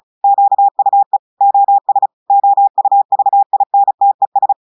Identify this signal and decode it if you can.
Morse code, heard as a keyed tone, about 36 wpm, 800 Hz. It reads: QUE OS OUVINTES
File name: portugais texte en clair 2 vitesse 35 mots minutes